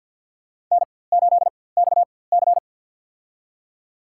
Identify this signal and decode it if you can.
Morse code, heard as a keyed tone, about 52 wpm, spaced as for 24 wpm, 700 Hz. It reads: N8XC
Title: USA Call Signs - 4x Character Spacing - 50wpm